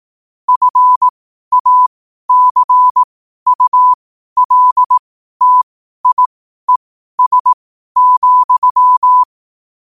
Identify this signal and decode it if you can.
Morse code, heard as a keyed tone, about 18 wpm, spaced as for 12 wpm, 1000 Hz. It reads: FACULTIES,
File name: qotdfiles / 12wpm